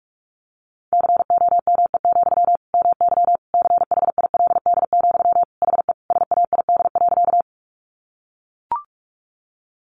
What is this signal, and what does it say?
Morse code, heard as a keyed tone, about 44 wpm, 700 Hz. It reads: COME, MY CHILD, HE SAID.
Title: Sentences from Top 800 Words 45wpm